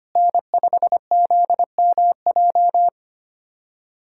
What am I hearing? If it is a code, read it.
Morse code, heard as a keyed tone, about 25 wpm, 700 Hz. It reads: N5ZMJ
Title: USA Callsign DE Callsign - 5 Characters - 25wpm